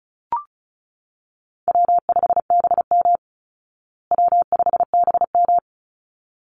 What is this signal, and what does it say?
Morse code, heard as a keyed tone, about 34 wpm, 700 Hz. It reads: W5BM W5BM
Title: Easy Call Signs Repeated 2x - 35wpm